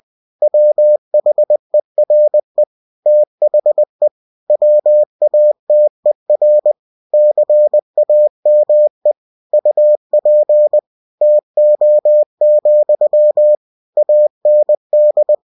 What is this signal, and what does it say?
Morse code, heard as a keyed tone, about 20 wpm, 600 Hz. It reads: WHERE THE WATER CAME UP TO, AND
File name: cw-books / winnie-the-pooh / winnie-20wpm-page-0096